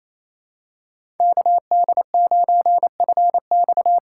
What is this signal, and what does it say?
Morse code, heard as a keyed tone, about 28 wpm, 700 Hz. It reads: KD9FX
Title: USA Call Signs - 6 Characters - 6x Character Spacing - 28wpm